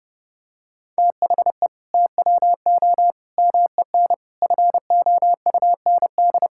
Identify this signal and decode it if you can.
Morse code, heard as a keyed tone, about 30 wpm, 700 Hz. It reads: THE TWO MEN FOUND
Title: Sentences from Top 1000 Words - Review Part 5 - 30wpm